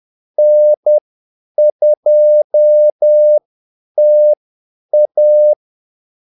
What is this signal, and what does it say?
Morse code, heard as a keyed tone, about 10 wpm, spaced as for 8 wpm, 600 Hz. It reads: N2TA
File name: Callsigns-10-8wpm